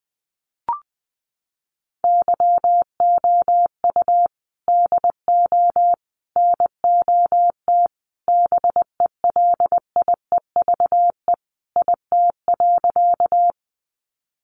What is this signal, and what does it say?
Morse code, heard as a keyed tone, about 20 wpm, 700 Hz. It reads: YOU DO NOT BELIEVE IT.
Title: Sentences from Top 1000 Words 20wpm